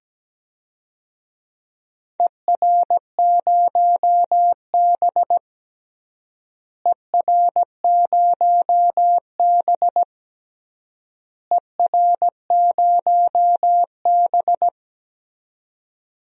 Morse code, heard as a keyed tone, about 17 wpm, 700 Hz. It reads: ER0B ER0B ER0B
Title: DXCC-100 Call Signs - 4 Characters - Repeated 3x - 17wpm